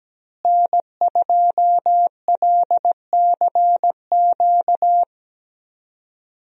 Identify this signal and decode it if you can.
Morse code, heard as a keyed tone, about 17 wpm, 700 Hz. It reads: N2LCQ
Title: USA Call Signs - 5 Characters - Repeated 2x - 17wpm